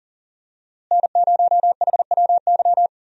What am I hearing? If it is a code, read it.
Morse code, heard as a keyed tone, about 38 wpm, 700 Hz. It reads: N0HWY
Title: USA Call Signs - 5 Characters - Repeated 3x - 40wpm